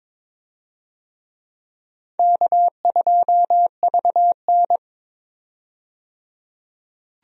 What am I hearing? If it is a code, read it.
Morse code, heard as a keyed tone, about 22 wpm, 700 Hz. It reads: K2VN
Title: USA Call Signs - 4 Characters - 22wpm